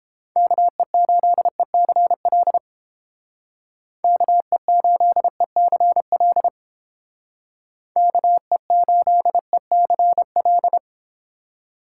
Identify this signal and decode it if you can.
Morse code, heard as a keyed tone, about 29 wpm, 700 Hz. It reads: KE8ECL KE8ECL KE8ECL